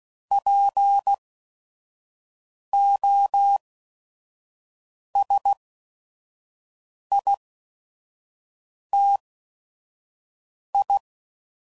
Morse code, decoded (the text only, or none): P O S I T I